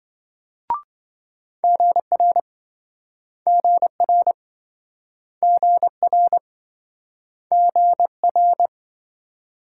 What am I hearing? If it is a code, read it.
Morse code, heard as a keyed tone, about 26 wpm, 700 Hz. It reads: GR GR GR GR